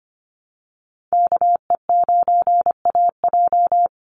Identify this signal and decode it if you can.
Morse code, heard as a keyed tone, about 25 wpm, 700 Hz. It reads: KE9AJ
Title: NA SOTA Chaser Callsigns 25wpm